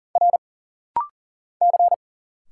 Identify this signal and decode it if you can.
Morse code, heard as a keyed tone, about 40 wpm, 700 Hz. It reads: R C
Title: Single Letters - Rapid-Fire 40wpm